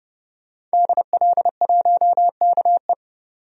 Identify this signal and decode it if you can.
Morse code, heard as a keyed tone, about 30 wpm, 700 Hz. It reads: DL1KE
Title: DXCC-100 Call Signs - 5 Characters - 30wpm